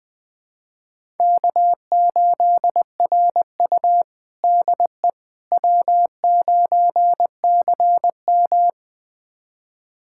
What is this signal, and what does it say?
Morse code, heard as a keyed tone, about 20 wpm, 700 Hz. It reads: K8RU DE W9CM
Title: USA Callsign DE Callsign - 4 Characters - 20wpm